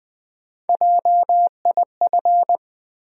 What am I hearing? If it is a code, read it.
Morse code, heard as a keyed tone, about 20 wpm, 700 Hz. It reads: JIF
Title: USA Call Sign Postfixes - 3 Letters 20wpm